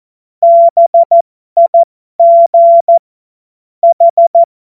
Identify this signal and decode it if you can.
Morse code, heard as a keyed tone, about 14 wpm, spaced as for 12 wpm, 680 Hz. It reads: BIG H
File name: Morsekurs / Witze